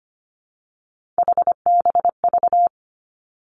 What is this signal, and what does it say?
Morse code, heard as a keyed tone, about 25 wpm, 700 Hz. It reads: HBV
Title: Three Challenging Letter-Number Combinations 25wpm